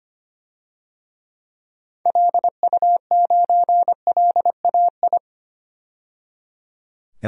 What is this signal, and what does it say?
Morse code, heard as a keyed tone, about 25 wpm, 700 Hz. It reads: LU9LAI
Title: DXCC-100 Call Signs - 6 Characters - 25wpm